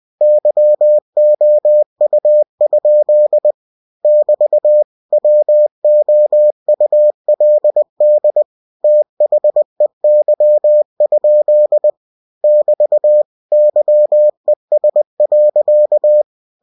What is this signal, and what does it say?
Morse code, heard as a keyed tone, about 20 wpm, 600 Hz. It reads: YOU? <BT> WOULD THEY? <BT> YES.
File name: cw-books / winnie-the-pooh / winnie-20wpm-page-0009